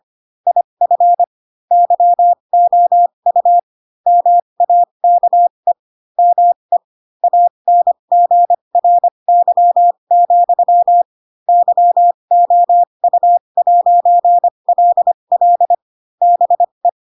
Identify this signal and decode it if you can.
Morse code, heard as a keyed tone, about 25 wpm, spaced as for 22 wpm, 700 Hz. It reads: IF YOU MAKE ME ANGRY, YOU'LL BE
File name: Pinocchio / Pinocchio ch4 18 25 ik1hge